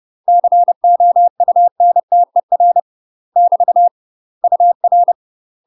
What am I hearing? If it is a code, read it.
Morse code, heard as a keyed tone, about 30 wpm, 700 Hz. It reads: COUNTER <BT> UR
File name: UK MASTER QSOs TRAINING / 30WPM / UK MASTER QSOs 0021